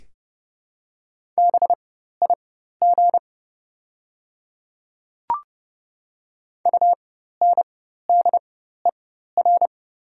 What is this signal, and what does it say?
Morse code, heard as a keyed tone, about 30 wpm, spaced as for 14 wpm, 690 Hz. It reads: BIG UNDER
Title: 4x Character Spacing - Top 500 Words - 30wpm